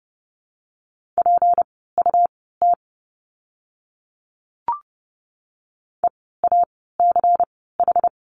Morse code, heard as a keyed tone, about 30 wpm, spaced as for 17 wpm, 700 Hz. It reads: PUT EACH
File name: course / 30:10wpm / 230 Lesson 37f - Letter x - Words 30wpm